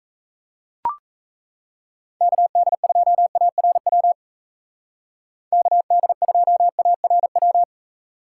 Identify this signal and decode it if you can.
Morse code, heard as a keyed tone, about 32 wpm, 700 Hz. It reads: KD2ARW KD2ARW
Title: Speed Racing - USA 2x3 Call Signs 28wpm